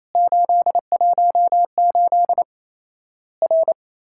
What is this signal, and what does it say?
Morse code, heard as a keyed tone, about 28 wpm, 700 Hz. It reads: 818 R
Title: Worked All Europe DX Contest - QTC Exchange - 28wpm